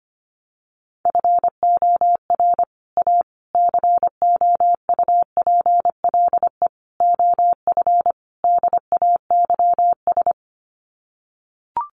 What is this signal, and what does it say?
Morse code, heard as a keyed tone, about 25 wpm, 700 Hz. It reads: FOR A COUPLE OF DAYS
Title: Sets of 5 Words Encore2 25wpm